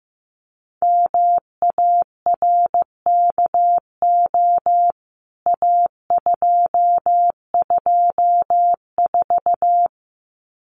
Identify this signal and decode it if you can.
Morse code, heard as a keyed tone, about 15 wpm, 700 Hz. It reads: MARKO A224
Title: CWT Exchange 15wpm